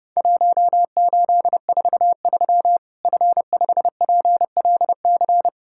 Morse code, heard as a keyed tone, about 30 wpm, 700 Hz. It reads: 1843 F5PLC
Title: Worked All Europe DX Contest - QTC Exchange - 30wpm